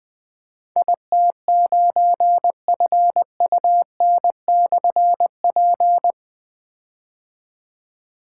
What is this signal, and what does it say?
Morse code, heard as a keyed tone, about 21 wpm, 700 Hz. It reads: IT9FUN/P
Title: Difficult Call Signs 20wpm